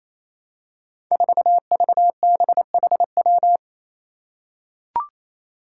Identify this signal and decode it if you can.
Morse code, heard as a keyed tone, about 29 wpm, 700 Hz. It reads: 4VBHW
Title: Five Challenging Letter-Number Combinations 28wpm